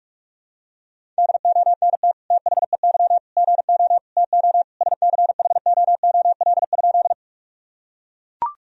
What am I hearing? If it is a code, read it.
Morse code, heard as a keyed tone, about 42 wpm, 700 Hz. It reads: DONT THEY GO TO SCHOOL?
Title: Sentences from Top 1000 Words - Review Part 1 - 45wpm